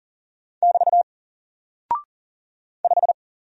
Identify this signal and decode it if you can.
Morse code, heard as a keyed tone, about 40 wpm, 700 Hz. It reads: <BT> 5
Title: Single Character (Letter:Number:Punctuation:Prosign) Alt 40wpm